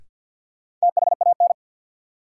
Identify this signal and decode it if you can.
Morse code, heard as a keyed tone, about 53 wpm, 700 Hz. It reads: THAN
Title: Sets of 2 Words - New Word Drill - 50wpm